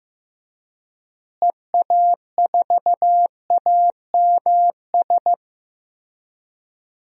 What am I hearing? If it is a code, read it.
Morse code, heard as a keyed tone, about 15 wpm, 700 Hz. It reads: EA4AMS
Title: DXCC-100 Call Signs - 6 Characters - Repeated 3x - 15wpm